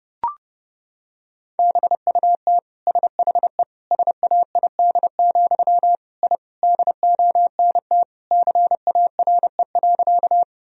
Morse code, heard as a keyed tone, about 30 wpm, 700 Hz. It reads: BUT SHE SAID, I DONT CARE.
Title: Sentences from Top 1000 Words - Review Part 1 - 30wpm